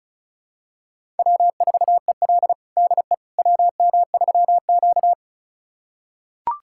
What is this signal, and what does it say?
Morse code, heard as a keyed tone, about 34 wpm, 700 Hz. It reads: W4EL DE WM3Q
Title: USA Callsign DE Callsign - 4 Characters - 35wpm